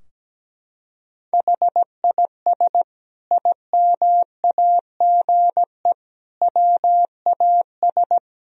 Morse code, heard as a keyed tone, about 17 wpm, 700 Hz. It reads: HIS IMAGE WAS
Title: Sentences from Top 1000 Words - Review Part 4 - 17wpm